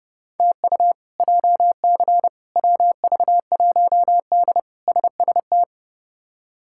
Morse code, heard as a keyed tone, about 30 wpm, 700 Hz. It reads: TU JC WV1D SST
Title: K1USN SST QSO Practice - 30wpm